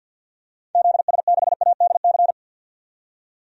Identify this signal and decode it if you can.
Morse code, heard as a keyed tone, about 52 wpm, 700 Hz. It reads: ZS6ADC
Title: DXCC-100 Call Signs - 6 Characters - 50wpm